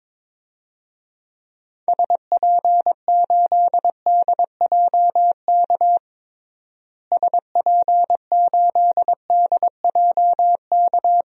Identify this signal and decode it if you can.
Morse code, heard as a keyed tone, about 22 wpm, 700 Hz. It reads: SP8DJK SP8DJK
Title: DXCC-100 Call Signs - 6 Characters - Repeated 3x - 22wpm